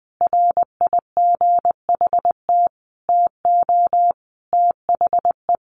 Morse code, heard as a keyed tone, about 20 wpm, 700 Hz. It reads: RIGHT TO THE
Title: Sentences from Top 1400 Words 20wpm